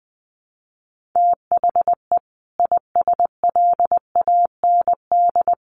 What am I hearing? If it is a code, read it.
Morse code, heard as a keyed tone, about 20 wpm, 700 Hz. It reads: THE ISLAND
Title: Sentences from Top 500 Words 20wpm